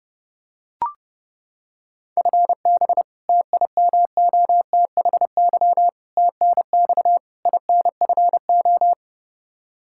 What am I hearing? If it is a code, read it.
Morse code, heard as a keyed tone, about 29 wpm, 700 Hz. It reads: FB TIMOTHY TNX INFO
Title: Practice QSO - Exchange 3 - More Info - 30wpm